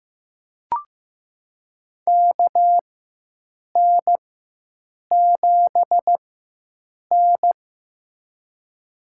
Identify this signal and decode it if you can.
Morse code, heard as a keyed tone, about 15 wpm, spaced as for 7 wpm, 700 Hz. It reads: KN7N
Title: USA Call Signs - 4 Characters - 4x Character Spacing - 15wpm